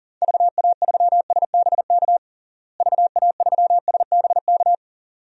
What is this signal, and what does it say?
Morse code, heard as a keyed tone, about 37 wpm, 700 Hz. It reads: VA3SBK VA3SBK
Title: DXCC-100 Call Signs - 6 Characters - Repeated 2x - 40wpm